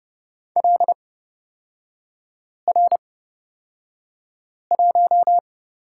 Morse code, heard as a keyed tone, about 30 wpm, spaced as for 5 wpm, 700 Hz. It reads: LR1
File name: Lesson 38e - Letter q - Three Characters 30wpm